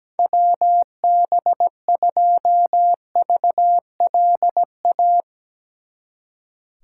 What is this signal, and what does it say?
Morse code, heard as a keyed tone, about 18 wpm, 700 Hz. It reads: WB2VLA
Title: Speed Racing - USA 2x3 Call Signs 17wpm